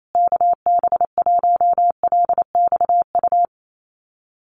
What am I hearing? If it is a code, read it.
Morse code, heard as a keyed tone, about 28 wpm, 700 Hz. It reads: KB1LXU
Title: USA Call Signs Repeated 3x 28wpm